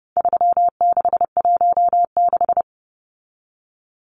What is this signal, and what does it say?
Morse code, heard as a keyed tone, about 30 wpm, 700 Hz. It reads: 3616
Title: Numbers - 4-Digit 30wpm